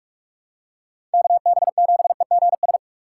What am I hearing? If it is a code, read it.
Morse code, heard as a keyed tone, about 46 wpm, 700 Hz. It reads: KB7EGS
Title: USA Call Signs - 6 Characters - 6x Character Spacing - 45wpm